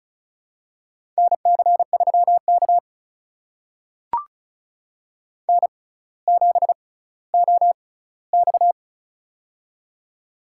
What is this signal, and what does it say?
Morse code, heard as a keyed tone, about 36 wpm, spaced as for 18 wpm, 700 Hz. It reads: NC3K N7OX
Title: USA Call Signs - 4 Characters - 6x Character Spacing - 35wpm